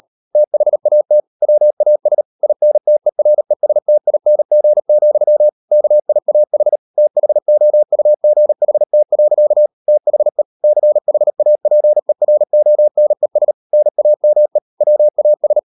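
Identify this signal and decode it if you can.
Morse code, heard as a keyed tone, about 38 wpm, 600 Hz. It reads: THAT WAS INTERESTING, KIAH THOUGHT. THE CHAPERONES NAME WAS
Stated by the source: Morse code practice at 38 WPM, sending Ken Follett's novel Never in English, from CW med Gnister.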